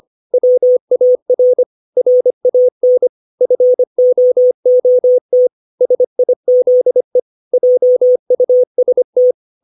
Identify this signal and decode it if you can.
Morse code, heard as a keyed tone, about 25 wpm, 500 Hz. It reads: WAR RAN FOOT SIZE JUST
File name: 632 words / 632 Words 25WPM